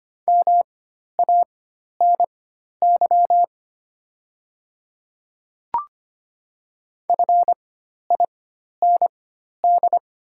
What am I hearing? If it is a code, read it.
Morse code, heard as a keyed tone, about 25 wpm, spaced as for 12 wpm, 700 Hz. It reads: MANY FIND
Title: 4x Character Spacing - Top 100 Words - 25wpm